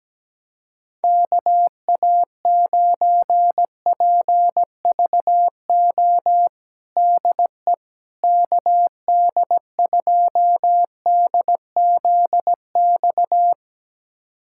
Morse code, heard as a keyed tone, about 17 wpm, 700 Hz. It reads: KA9PVO DE KD2DZX
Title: USA Callsign DE Callsign - 6 Characters - 17wpm